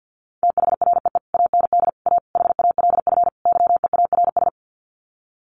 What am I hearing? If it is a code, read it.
Morse code, heard as a keyed tone, about 48 wpm, 700 Hz. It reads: THREE AND A HALF YEARS
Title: Sets of 5 Words Encore2 50wpm